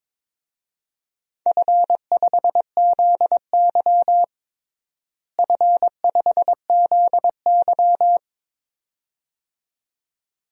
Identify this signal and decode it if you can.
Morse code, heard as a keyed tone, about 22 wpm, 700 Hz. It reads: F5ZY F5ZY
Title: DXCC-100 Call Signs - 4 Characters - Repeated 2x - 22wpm